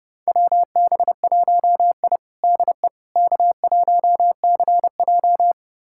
Morse code, heard as a keyed tone, about 30 wpm, 700 Hz. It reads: WB1I DE K1CJ
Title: USA Callsign DE Callsign - 4 Characters - 30wpm